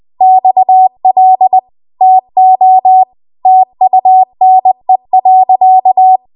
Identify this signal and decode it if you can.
Morse code, heard as a keyed tone, about 20 wpm, 750 Hz. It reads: XL TO TUNE.
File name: Archive / 20 WPM / 20WPM